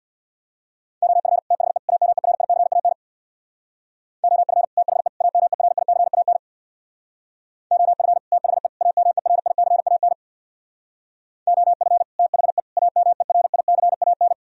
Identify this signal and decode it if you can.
Morse code, heard as a keyed tone, about 64 wpm, 700 Hz. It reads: OF THE AMERICAN OF THE AMERICAN OF THE AMERICAN OF THE AMERICAN